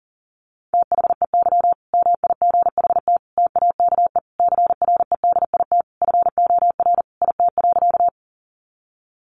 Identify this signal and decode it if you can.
Morse code, heard as a keyed tone, about 40 wpm, 700 Hz. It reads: THEY MIGHT TAKE CREDIT FOR IT.
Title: Sentences from Top 1000 Words 40wpm